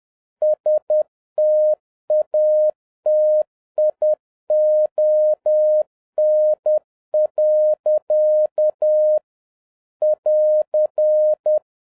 Morse code, heard as a keyed tone, about 10 wpm, 610 Hz. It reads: STATION. <AR>
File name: 2011-07-mp3 / n36 THE 3.11 2 10wpm